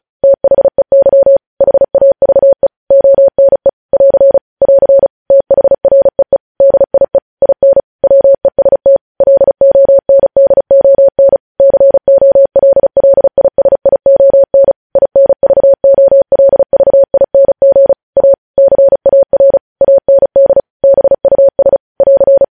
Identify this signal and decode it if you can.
Morse code, heard as a keyed tone, about 36 wpm, 550 Hz. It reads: THEY HAVE ONE <AR> <AR> THREE DIE IN WEST LONDON COLLISION INVOLVING A CAR AND BUS <AR>
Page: news headlines in morse code at 35 wpm